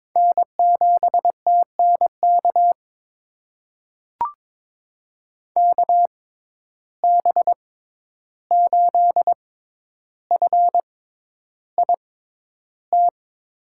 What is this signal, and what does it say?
Morse code, heard as a keyed tone, about 22 wpm, spaced as for 11 wpm, 700 Hz. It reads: N7TNK KB8FIT
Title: USA Call Signs - 6x Character Spacing - 22wpm